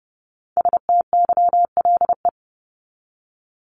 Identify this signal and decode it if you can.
Morse code, heard as a keyed tone, about 30 wpm, 700 Hz. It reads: STYLE
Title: Top 751-1000 Words 30wpm